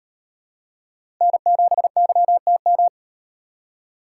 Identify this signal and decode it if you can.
Morse code, heard as a keyed tone, about 38 wpm, 700 Hz. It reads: N7YTM